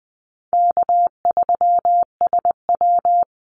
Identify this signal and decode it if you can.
Morse code, heard as a keyed tone, about 20 wpm, 700 Hz. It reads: K3SW
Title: Easy Call Signs Repeated 2x - 20wpm